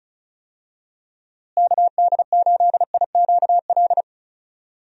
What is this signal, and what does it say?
Morse code, heard as a keyed tone, about 34 wpm, 700 Hz. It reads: KD8IQL
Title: USA Call Signs - 6 Characters - Repeated 3x - 35wpm